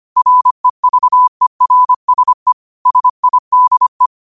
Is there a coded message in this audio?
Yes: Morse code from 0.2 to 4.1 s, heard as a keyed tone, about 25 wpm, 1000 Hz. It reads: REVERSE SIDE